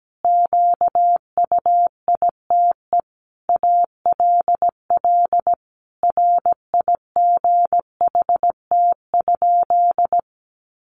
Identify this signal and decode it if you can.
Morse code, heard as a keyed tone, about 17 wpm, 700 Hz. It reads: QUITE ALL RIGHT?
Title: Sentences from Top 700 Words 17wpm